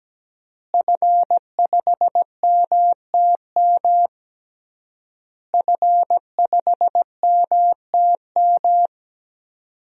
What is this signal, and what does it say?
Morse code, heard as a keyed tone, about 17 wpm, 700 Hz. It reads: F5MTM F5MTM
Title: DXCC-100 Call Signs - 5 Characters - Repeated 3x - 17wpm